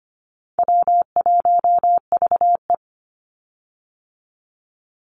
Morse code, heard as a keyed tone, about 25 wpm, 700 Hz. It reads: W1VE
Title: Easy Call Signs - 25wpm